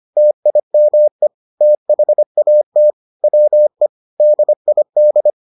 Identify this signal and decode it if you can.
Morse code, heard as a keyed tone, about 25 wpm, 600 Hz. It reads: TIME THAT WE DID